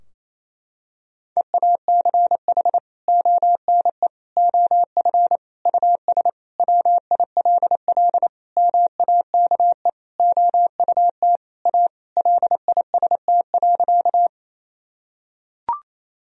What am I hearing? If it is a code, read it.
Morse code, heard as a keyed tone, about 27 wpm, 700 Hz. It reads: EACH ONE OF US WILL MAKE OUT A LIST.
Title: Sentences from Top 1000 Words - Review Part 2 - 28wpm